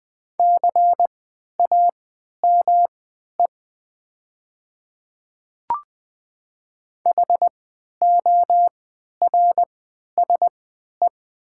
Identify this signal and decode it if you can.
Morse code, heard as a keyed tone, about 20 wpm, spaced as for 11 wpm, 690 Hz. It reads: CAME HORSE
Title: 3x Character Spacing - Top 300 Words - 20wpm